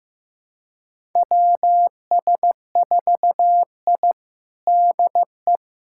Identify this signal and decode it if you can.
Morse code, heard as a keyed tone, about 15 wpm, 700 Hz. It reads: WS4I DE
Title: USA Callsign DE Callsign - 4 Characters - 15wpm